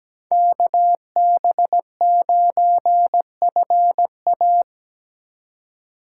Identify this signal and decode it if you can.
Morse code, heard as a keyed tone, about 21 wpm, 700 Hz. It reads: KB9FA
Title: Speed Racing - USA 2x2 Call Signs 17wpm